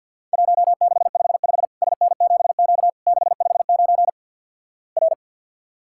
Morse code, heard as a keyed tone, about 50 wpm, 700 Hz. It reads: 1655 SN7Q 659 R
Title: Worked All Europe DX Contest - QTC Exchange - 50wpm